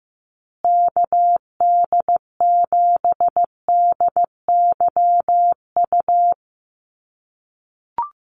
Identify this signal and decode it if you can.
Morse code, heard as a keyed tone, about 15 wpm, 700 Hz. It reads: KD7DYU
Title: USA Call Signs Repeated 3x 15wpm